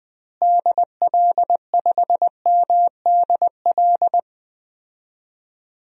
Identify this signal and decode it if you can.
Morse code, heard as a keyed tone, about 20 wpm, 700 Hz. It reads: DL5MDL
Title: DXCC-100 Call Signs - 6 Characters - Repeated 2x - 20wpm